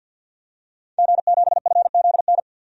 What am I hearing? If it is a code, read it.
Morse code, heard as a keyed tone, about 50 wpm, 700 Hz. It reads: G7FZN
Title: DXCC-100 Call Signs - 5 Characters - Repeated 2x - 50wpm